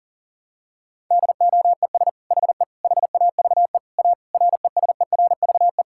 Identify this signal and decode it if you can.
Morse code, heard as a keyed tone, about 40 wpm, 700 Hz. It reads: DOES HE HAVE A RESERVE
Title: Sentences from Top 1000 Words - Review Part 6 - 40wpm